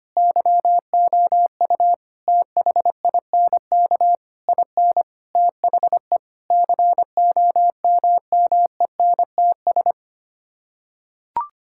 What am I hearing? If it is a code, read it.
Morse code, heard as a keyed tone, about 25 wpm, 700 Hz. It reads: YOU THINK IN THE COMMENTS
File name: Sets of 9 Words - 25wpm